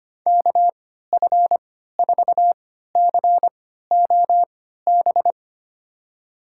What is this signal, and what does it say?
Morse code, heard as a keyed tone, about 25 wpm, spaced as for 14 wpm, 700 Hz. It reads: KF4COB
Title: USA Call Signs - 6 Characters - 3x Character Spacing - 25wpm